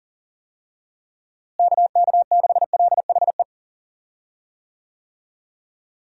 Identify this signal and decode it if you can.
Morse code, heard as a keyed tone, about 40 wpm, 700 Hz. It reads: KK6LHE
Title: USA Call Signs - 6 Characters - Repeated 2x - 40wpm